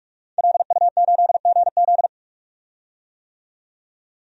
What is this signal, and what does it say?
Morse code, heard as a keyed tone, about 46 wpm, 700 Hz. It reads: LU8GZ
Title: DXCC-100 Call Signs - 5 Characters - 45wpm